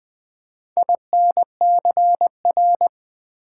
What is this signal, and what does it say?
Morse code, heard as a keyed tone, about 20 wpm, 700 Hz. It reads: INCR
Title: Build A Word - 4 Letters At A Time - 10-Letter Words - 20wpm